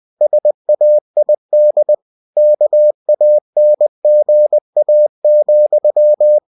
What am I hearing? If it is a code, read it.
Morse code, heard as a keyed tone, about 20 wpm, 600 Hz. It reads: SAID KANGA,
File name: cw-books / winnie-the-pooh / winnie-20wpm-page-0073